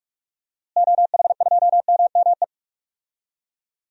Morse code, heard as a keyed tone, about 46 wpm, 700 Hz. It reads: OH2MME
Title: DXCC-100 Call Signs - 6 Characters - Repeated 3x - 45wpm